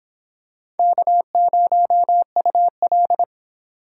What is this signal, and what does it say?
Morse code, heard as a keyed tone, about 25 wpm, 700 Hz. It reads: K0UL